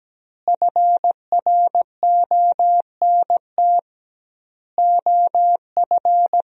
Morse code, heard as a keyed tone, about 17 wpm, 700 Hz. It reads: FRONT OF
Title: 2x Word-Spacing - Sentences from Top 500 Words - 17wpm